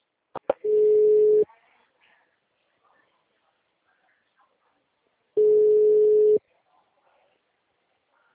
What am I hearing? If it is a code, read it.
Ringback tone (Europe, 425 Hz).
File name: out- -2022- - - .43390